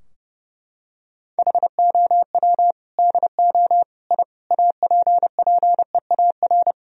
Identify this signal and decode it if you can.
Morse code, heard as a keyed tone, about 30 wpm, 700 Hz. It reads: HOW DO I APPEAR
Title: Sentences from Top 1000 Words - Review Part 2 - 30wpm